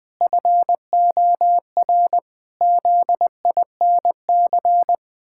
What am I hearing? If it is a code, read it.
Morse code, heard as a keyed tone, about 19 wpm, 700 Hz. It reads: FOR ZINC
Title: Pangram Challenge v2 - 20wpm